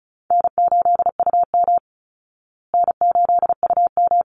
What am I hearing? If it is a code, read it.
Morse code, heard as a keyed tone, about 34 wpm, 700 Hz. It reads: N8UM N8UM
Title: Easy Call Signs Repeated 3x - 35wpm